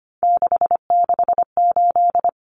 Morse code, heard as a keyed tone, about 25 wpm, 700 Hz. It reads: 668
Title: Numbers - 3-Digit 25wpm